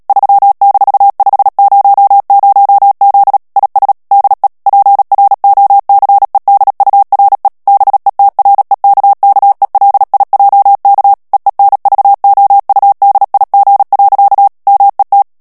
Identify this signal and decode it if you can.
Morse code, heard as a keyed tone, about 37 wpm, 800 Hz. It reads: 3-500Z IS DE PROCEDURE BETREKKELIJK EENVOUDIG. MET
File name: morse mp3 / HOLLANDAIS texte clair 2 vitesse 35 mots minute